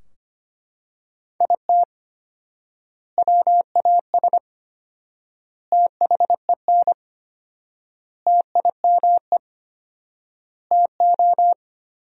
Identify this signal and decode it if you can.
Morse code, heard as a keyed tone, about 25 wpm, 700 Hz. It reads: IT WAS THEN TIME TO
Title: 4x Word-Spacing - Sentences from Top 300 Words - 25wpm